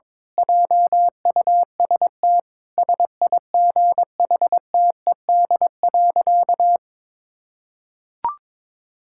Morse code, heard as a keyed tone, about 22 wpm, 700 Hz. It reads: JUST SIGHTED.
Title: Pangram Challenge v2 - 22wpm